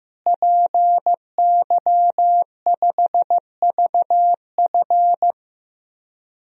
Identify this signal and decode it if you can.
Morse code, heard as a keyed tone, about 16 wpm, 700 Hz. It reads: PY5VF
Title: DXCC-100 Call Signs - 5 Characters - Repeated 3x - 15wpm